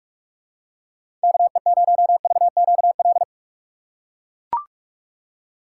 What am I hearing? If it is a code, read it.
Morse code, heard as a keyed tone, about 43 wpm, 700 Hz. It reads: KE0VQL
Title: USA Call Signs - 3x Character Spacing - 45wpm